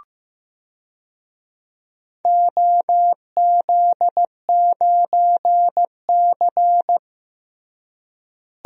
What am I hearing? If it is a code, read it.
Morse code, heard as a keyed tone, about 16 wpm, 700 Hz. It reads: OZ9C
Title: DXCC-100 Call Signs - 4 Characters - Repeated 2x - 15wpm